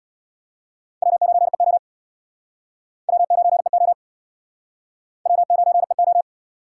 Morse code, heard as a keyed tone, about 72 wpm, 700 Hz. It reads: W9EO W9EO W9EO